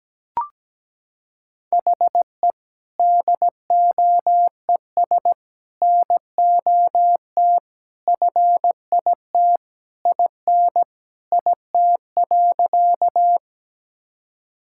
Morse code, heard as a keyed tone, about 17 wpm, 700 Hz. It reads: HE DOES NOT FIT IN IT.
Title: Sentences from Top 1000 Words - Review Part 5 - 17wpm